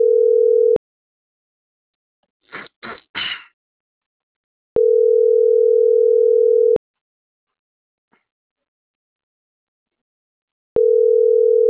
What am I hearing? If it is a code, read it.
Ringback tone (North America), 440 + 480 Hz.